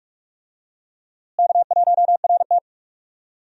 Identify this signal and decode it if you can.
Morse code, heard as a keyed tone, about 45 wpm, 700 Hz. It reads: K1RT
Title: USA Call Signs - 4 Characters - Repeated 3x - 45wpm